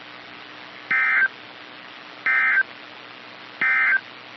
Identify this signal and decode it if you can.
Emergency Alert System (SAME) tones, heard as harsh buzzing bursts. It is heard from 0.8 to 4.1 s.
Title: NOAA Weather Radio - KEC55